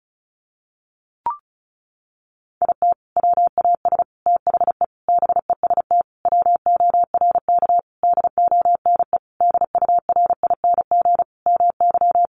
Morse code, heard as a keyed tone, about 34 wpm, 700 Hz. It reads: IT WAS THE BEST WORK DONE DURING MY
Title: Sentences from Top 1300 Words 35wpm